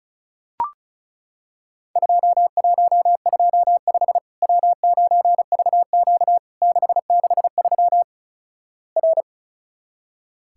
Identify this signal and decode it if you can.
Morse code, heard as a keyed tone, about 36 wpm, 700 Hz. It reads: 2125 W9VQ 663 R
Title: Worked All Europe DX Contest - QTC Exchange - 35wpm